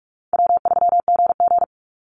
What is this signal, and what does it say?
Morse code, heard as a keyed tone, about 44 wpm, 700 Hz. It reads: W3GG
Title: Easy Call Signs Repeated 2x - 45wpm